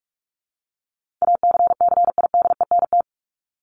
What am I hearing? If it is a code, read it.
Morse code, heard as a keyed tone, about 44 wpm, 700 Hz. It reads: ACCIDENT
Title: Top 1751-2000 Words 45wpm